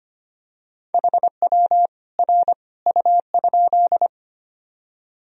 Morse code, heard as a keyed tone, about 25 wpm, 700 Hz. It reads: HW R U?
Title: Sentences from Top 100 Words and Abbreviations 25wpm